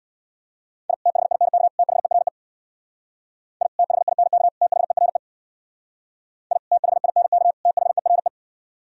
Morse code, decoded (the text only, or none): I THINK THERE I THINK THERE I THINK THERE